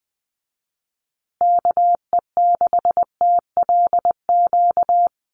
Morse code, heard as a keyed tone, about 20 wpm, 700 Hz. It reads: KE6TLQ
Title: USA Call Signs Repeated 2x 20wpm